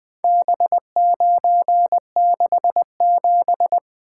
Morse code, heard as a keyed tone, about 20 wpm, 700 Hz. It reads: B967
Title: Four Letter-Number Combinations - 20wpm